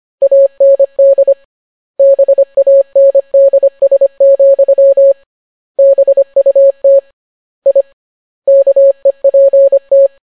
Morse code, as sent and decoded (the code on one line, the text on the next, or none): .- -. -.. / -... .- -. -.. ... --..-- / -... ..- - / .. / -.- . .--. -
AND BANDS, BUT I KEPT